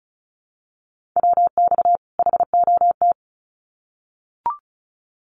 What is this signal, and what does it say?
Morse code, heard as a keyed tone, about 34 wpm, 700 Hz. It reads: WX HOT
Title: Common QSO Phrases 35wpm